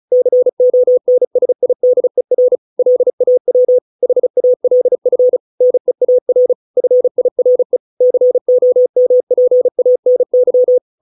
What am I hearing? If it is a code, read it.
Morse code, heard as a keyed tone, about 34 wpm, 500 Hz. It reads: CONSIDER LAW HALF NEAR FIRE COMPANY